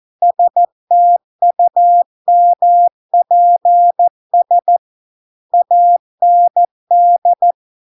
Morse code, heard as a keyed tone, about 14 wpm, 700 Hz. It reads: STUMPS AND